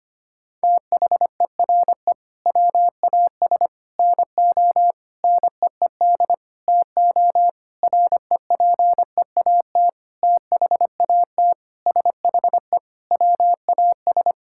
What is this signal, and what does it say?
Morse code, heard as a keyed tone, about 25 wpm, 700 Hz. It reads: THERE WAS NO NEED TO REPEAT THAT SHE WAS
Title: Sentences from Top 1600 Words 25wpm